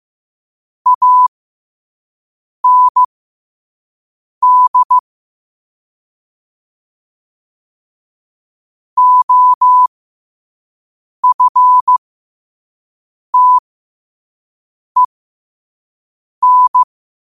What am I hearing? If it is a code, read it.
Morse code, heard as a keyed tone, about 15 wpm, spaced as for 5 wpm, 1000 Hz. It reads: AND OFTEN
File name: qotdfiles / 5wpm